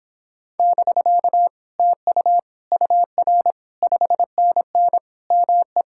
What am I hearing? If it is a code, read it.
Morse code, heard as a keyed tone, about 26 wpm, 700 Hz. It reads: <BK> TU UR 5NN ME